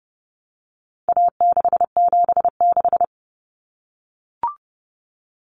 Morse code, heard as a keyed tone, about 30 wpm, 700 Hz. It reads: A676
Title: Common Cut Numbers - 4-Digit 30wpm